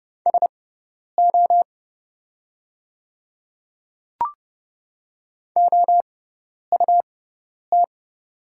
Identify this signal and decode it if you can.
Morse code, heard as a keyed tone, about 30 wpm, spaced as for 10 wpm, 700 Hz. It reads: SO OUT
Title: 6x Character Spacing - Top 100 Words - 30wpm